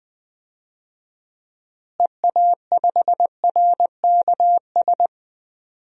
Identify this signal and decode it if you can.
Morse code, heard as a keyed tone, about 20 wpm, 700 Hz. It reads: EA5RKS
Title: DXCC-100 Call Signs - 6 Characters - Repeated 2x - 20wpm